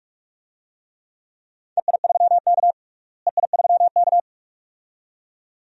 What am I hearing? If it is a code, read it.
Morse code, heard as a keyed tone, about 46 wpm, 700 Hz. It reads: EI3K EI3K
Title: DXCC-100 Call Signs - 4 Characters - Repeated 2x - 45wpm